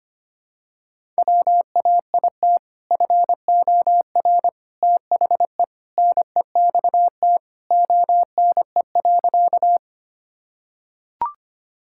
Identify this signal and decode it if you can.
Morse code, heard as a keyed tone, about 25 wpm, 700 Hz. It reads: WAIT FOR THE NEXT ONE.
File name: practice / 25wpm / Sentences with 2-Word Phrases - 25wpm